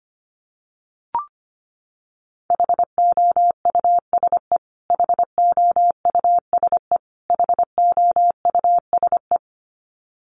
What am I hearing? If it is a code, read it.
Morse code, heard as a keyed tone, about 25 wpm, 700 Hz. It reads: HOUSE HOUSE HOUSE
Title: Top 200 Words Repeated 25wpm